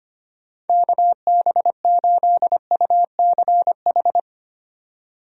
Morse code, heard as a keyed tone, about 25 wpm, 700 Hz. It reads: KB8UCH
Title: USA Call Signs - 6 Characters - 25wpm